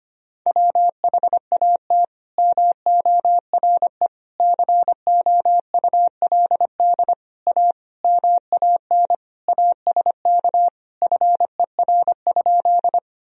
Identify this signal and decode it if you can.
Morse code, heard as a keyed tone, about 25 wpm, 700 Hz. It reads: WHAT MORE COULD A MAN ASK FER?
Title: Sentences from Top 200 Words and Abbreviations 25wpm